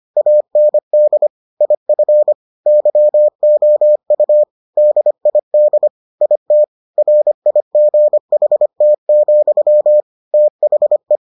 Morse code, heard as a keyed tone, about 25 wpm, 600 Hz. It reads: AND IF YOU DID IT RIGHT, THE